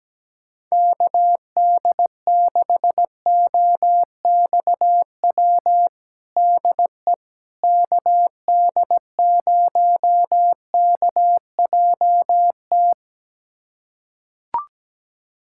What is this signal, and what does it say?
Morse code, heard as a keyed tone, about 17 wpm, 700 Hz. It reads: KD6OXW DE KD0KJT
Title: USA Callsign DE Callsign - 6 Characters - 17wpm